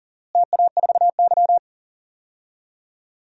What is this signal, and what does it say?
Morse code, heard as a keyed tone, about 40 wpm, 700 Hz. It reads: TA4Y
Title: DXCC-100 Call Signs - 4 Characters - Repeated 3x - 40wpm